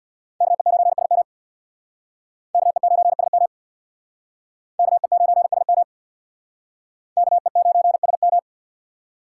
Morse code, heard as a keyed tone, about 58 wpm, 700 Hz. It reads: KE9SM KE9SM KE9SM KE9SM
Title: Speed Racing - USA 2x2 Call Signs 50wpm